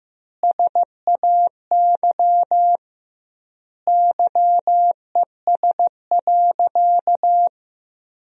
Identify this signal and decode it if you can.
Morse code, heard as a keyed tone, about 15 wpm, 700 Hz. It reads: SAY YES.
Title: 2x Word-Spacing - Sentences from Top 500 Words - 15wpm